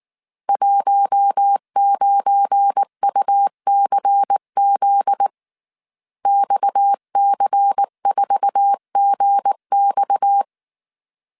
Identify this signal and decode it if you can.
Morse code, heard as a keyed tone, about 19 wpm, 800 Hz. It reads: 19UCZ XC4GX
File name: CW / DET / 1 9 1500C S A 18wpm 2459